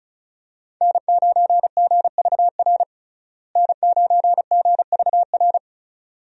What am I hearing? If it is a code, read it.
Morse code, heard as a keyed tone, about 36 wpm, 700 Hz. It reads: N9GVR N9GVR
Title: USA Call Signs - 5 Characters - Repeated 2x - 35wpm